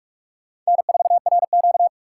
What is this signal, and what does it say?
Morse code, heard as a keyed tone, about 43 wpm, 700 Hz. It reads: N4RQ